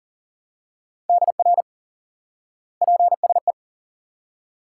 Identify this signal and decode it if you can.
Morse code, heard as a keyed tone, about 40 wpm, 700 Hz. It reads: DR PSE
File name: IQE Course 3 - Lesson 89 - YAGI - Two Elements - 40wpm